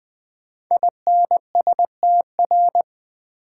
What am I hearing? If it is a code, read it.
Morse code, heard as a keyed tone, about 20 wpm, 700 Hz. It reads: INSTR
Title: Build A Word - 5 Letters At A Time - 10-Letter Words - 20wpm